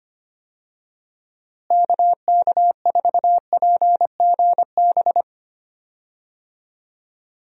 Morse code, heard as a keyed tone, about 25 wpm, 700 Hz. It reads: KK4PGB
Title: USA Call Signs - 6 Characters - Repeated 2x - 25wpm